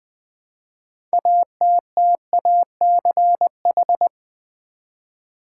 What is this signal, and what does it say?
Morse code, heard as a keyed tone, about 20 wpm, 700 Hz. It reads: ATTACH
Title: 6 Letter Words - First 3 Letters Given - 20wpm